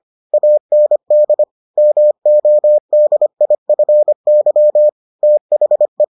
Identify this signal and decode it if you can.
Morse code, heard as a keyed tone, about 25 wpm, 600 Hz. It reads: AND MODIFY THE